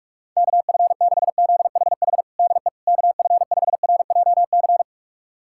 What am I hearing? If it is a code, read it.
Morse code, heard as a keyed tone, about 46 wpm, 700 Hz. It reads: KF6ZHH DE KF5RJC
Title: USA Callsign DE Callsign 45wpm